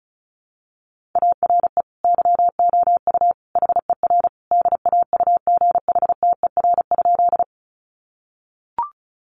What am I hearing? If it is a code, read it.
Morse code, heard as a keyed tone, about 34 wpm, 700 Hz. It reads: ARE YOU HER DAUGHTER?
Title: Sentences from Top 1500 Words 35wpm